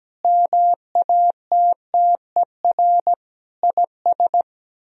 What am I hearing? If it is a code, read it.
Morse code, heard as a keyed tone, about 17 wpm, 700 Hz. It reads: MATTER IS
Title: Speed-Racing - Sets of 5 Words 17wpm